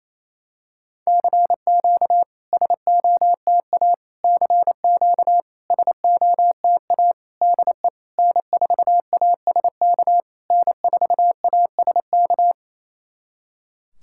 Morse code, heard as a keyed tone, about 29 wpm, 700 Hz. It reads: CQ SOTA CQ SOTA DE N4ASK N4ASK
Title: SOTA QSO Practice 28wpm